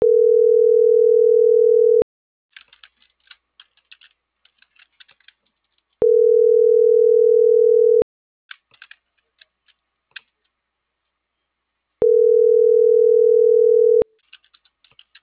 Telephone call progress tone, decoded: ringback tone

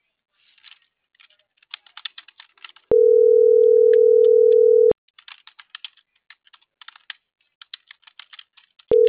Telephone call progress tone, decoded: ringback tone

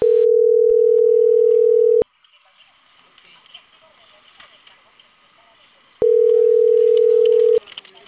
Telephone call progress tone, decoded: ringback tone

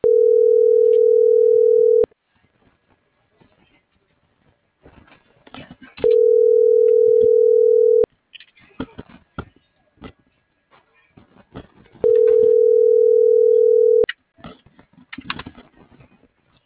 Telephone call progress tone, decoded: ringback tone

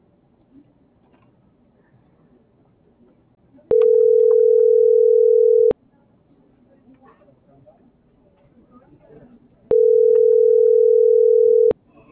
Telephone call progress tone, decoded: ringback tone

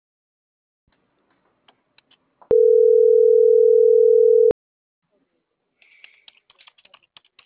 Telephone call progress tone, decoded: ringback tone

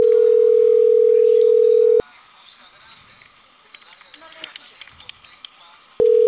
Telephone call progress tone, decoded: ringback tone

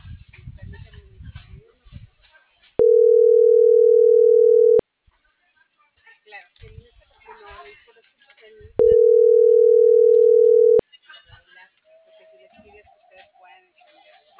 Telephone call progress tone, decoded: ringback tone

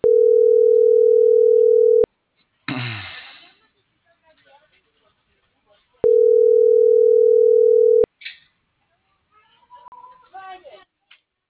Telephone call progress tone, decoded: ringback tone

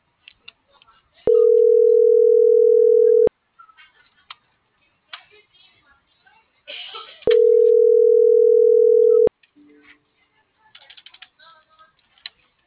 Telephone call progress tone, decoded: ringback tone

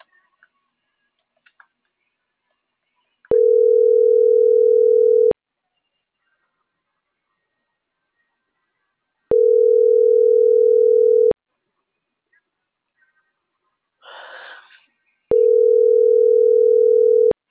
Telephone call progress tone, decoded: ringback tone

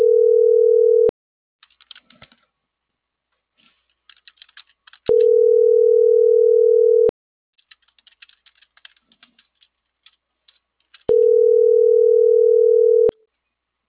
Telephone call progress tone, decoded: ringback tone

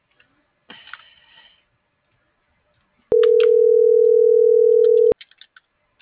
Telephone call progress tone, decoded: ringback tone

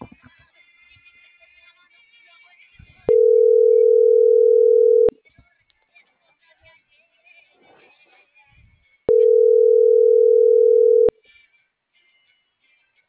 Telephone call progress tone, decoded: ringback tone

